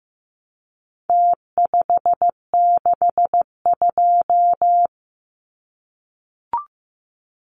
Morse code, heard as a keyed tone, about 15 wpm, 700 Hz. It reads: T562
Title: Numbers - Contest Style - 4-Digit 15wpm